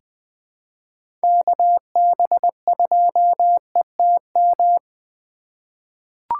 Morse code, heard as a keyed tone, about 20 wpm, 700 Hz. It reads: KB2ETM
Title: USA Call Signs - 3x Character Spacing - 20wpm